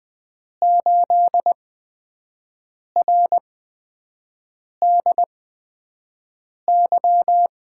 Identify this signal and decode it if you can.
Morse code, heard as a keyed tone, about 20 wpm, spaced as for 5 wpm, 700 Hz. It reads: 8RDY
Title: USA Call Signs - 6 Characters - 8x Character Spacing - 20wpm